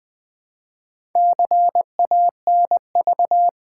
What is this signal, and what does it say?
Morse code, heard as a keyed tone, about 20 wpm, 700 Hz. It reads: CANV
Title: Build A Word - 2 Letters At A Time - 6-Letter Words - 20wpm